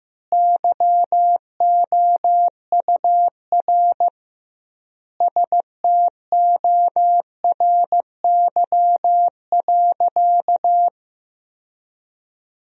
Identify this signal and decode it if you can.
Morse code, heard as a keyed tone, about 15 wpm, 700 Hz. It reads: YOUR STORY.
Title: 2x Word-Spacing - Sentences from Top 300 Words - 15wpm